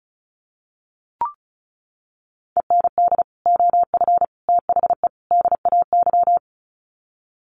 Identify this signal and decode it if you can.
Morse code, heard as a keyed tone, about 34 wpm, 700 Hz. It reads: END OF THE DAY
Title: Sets of 4 Words 35wpm